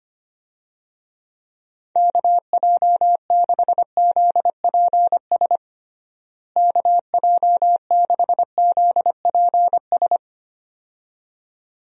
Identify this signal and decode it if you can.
Morse code, heard as a keyed tone, about 25 wpm, 700 Hz. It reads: KJ6ZPS KJ6ZPS
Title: USA Call Signs - 6 Characters - Repeated 2x - 25wpm